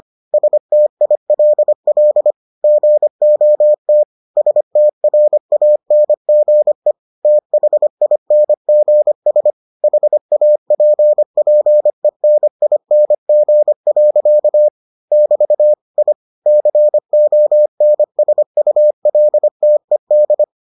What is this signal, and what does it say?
Morse code, heard as a keyed tone, about 25 wpm, 600 Hz. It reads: STILL GOT STRANGE THINGS HAPPENING. <BT> I CONSULTED